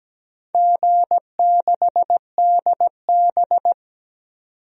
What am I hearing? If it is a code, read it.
Morse code, heard as a keyed tone, about 17 wpm, 700 Hz. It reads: G6DB
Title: DXCC-100 Call Signs - 4 Characters - 17wpm